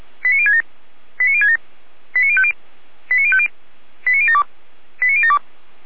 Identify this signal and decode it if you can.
Five-tone selective call (ZVEI), heard as quick runs of five tones, tones of about 80 ms. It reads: 89067 89067 89050 89050 89072 89072